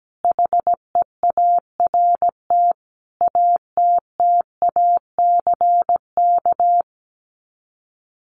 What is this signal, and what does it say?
Morse code, heard as a keyed tone, about 17 wpm, 700 Hz. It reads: HEART ATTACK
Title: Sets of 5 Words Encore2 17wpm